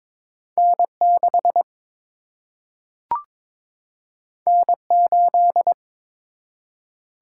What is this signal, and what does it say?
Morse code, heard as a keyed tone, about 21 wpm, 700 Hz. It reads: N 6 N 8
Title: Single-Letter Single-Number 22wpm